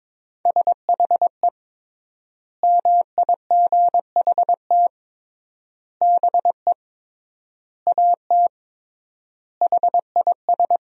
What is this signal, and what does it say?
Morse code, heard as a keyed tone, about 22 wpm, 700 Hz. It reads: SHE MIGHT BE AT HIS
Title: 3x Word-Spacing - Sentences from Top 300 Words - 22wpm